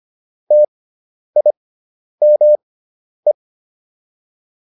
Morse code, heard as a keyed tone, about 25 wpm, spaced as for 10 wpm, 600 Hz. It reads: TIME